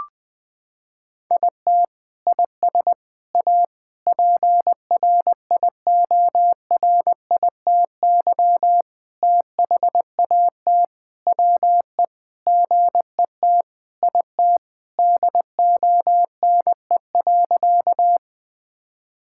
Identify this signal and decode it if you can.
Morse code, heard as a keyed tone, about 21 wpm, 700 Hz. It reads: IT IS A PRIORITY THAT WE GET IT DONE.
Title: Sentences from Top 1600 Words 20wpm